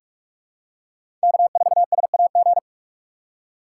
Morse code, heard as a keyed tone, about 46 wpm, 700 Hz. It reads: K4SAG
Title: USA Call Signs - 6x Character Spacing - 45wpm